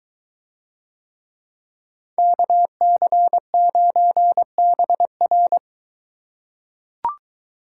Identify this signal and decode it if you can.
Morse code, heard as a keyed tone, about 23 wpm, 700 Hz. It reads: KC9BR